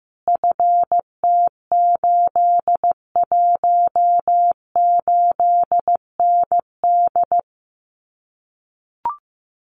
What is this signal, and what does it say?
Morse code, heard as a keyed tone, about 15 wpm, 700 Hz. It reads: FT818ND
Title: Common QSO Phrases Encore 15wpm